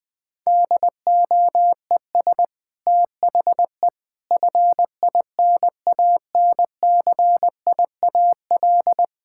Morse code, heard as a keyed tone, about 20 wpm, 700 Hz. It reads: DOES THE FINANCIAL
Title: Sentences from Top 1000 Words - Review Part 5 - 20wpm